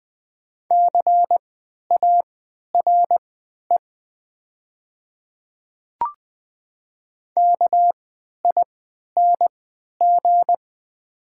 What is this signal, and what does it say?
Morse code, heard as a keyed tone, about 20 wpm, spaced as for 11 wpm, 700 Hz. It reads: CARE KING
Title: 3x Character Spacing - Top 500 Words - 20wpm